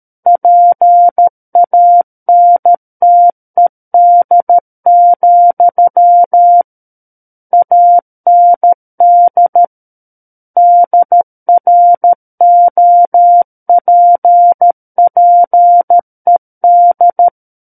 Morse code, heard as a keyed tone, about 13 wpm, 700 Hz. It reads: PANTED, AND DROPPED
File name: Book One / 04-Chapter 4 - 13 WPM